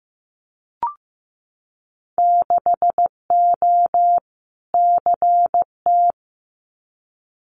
Morse code, heard as a keyed tone, about 15 wpm, 700 Hz. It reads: 6O CT
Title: Winter Field Day Exchanges Encore 15wpm